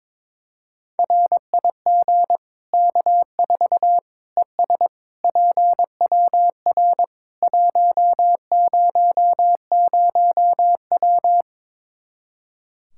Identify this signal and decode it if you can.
Morse code, heard as a keyed tone, about 22 wpm, 700 Hz. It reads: RIG K4 ES PWR 100W
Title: Practice QSO - Exchange 2 - Info - 22wpm